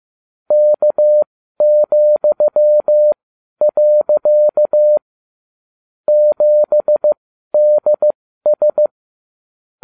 Morse code, heard as a keyed tone, about 15 wpm, spaced as for 12 wpm, 600 Hz. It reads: K,. 7DS